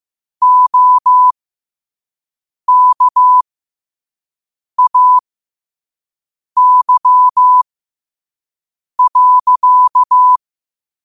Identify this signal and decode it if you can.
Morse code, heard as a keyed tone, about 15 wpm, spaced as for 5 wpm, 1000 Hz. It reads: OKAY.
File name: qotdfiles / 5wpm